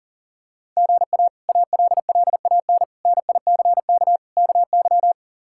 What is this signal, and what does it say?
Morse code, heard as a keyed tone, about 37 wpm, 700 Hz. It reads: GA ALLAN NICK KY
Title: K1USN SST QSO Practice - 40wpm